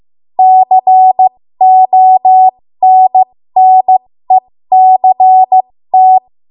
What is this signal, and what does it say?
Morse code, heard as a keyed tone, about 15 wpm, spaced as for 13 wpm, 750 Hz. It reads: CONNECT
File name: Morse / 13WPM